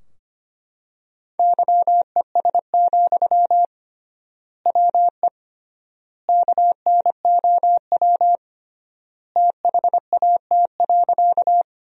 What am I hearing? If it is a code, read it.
Morse code, heard as a keyed tone, about 25 wpm, 700 Hz. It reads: YES, WE KNOW THAT.
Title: 3x Word-Spacing - Sentences from Top 500 Words - 25wpm